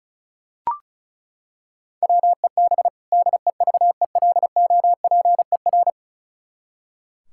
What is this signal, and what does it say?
Morse code, heard as a keyed tone, about 36 wpm, 700 Hz. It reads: WEB DEVELOPER
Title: Practice QSO - Exchange 3 - More Info - 35wpm